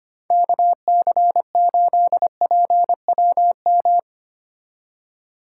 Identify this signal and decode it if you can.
Morse code, heard as a keyed tone, about 32 wpm, 700 Hz. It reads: KC8PWM